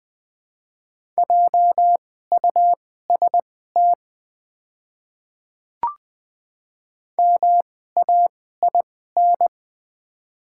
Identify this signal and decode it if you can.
Morse code, heard as a keyed tone, about 20 wpm, spaced as for 14 wpm, 700 Hz. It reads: JUST MAIN
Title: 2x Character Spacing - Top 300 Words - 20wpm